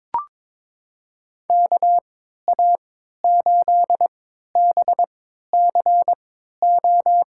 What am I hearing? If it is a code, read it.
Morse code, heard as a keyed tone, about 22 wpm, spaced as for 13 wpm, 700 Hz. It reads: KA8BCO
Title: USA Call Signs - 6 Characters - 3x Character Spacing - 22wpm